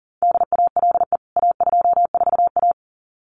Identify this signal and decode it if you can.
Morse code, heard as a keyed tone, about 40 wpm, 700 Hz. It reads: DALE A24A
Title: CWT Exchange 40wpm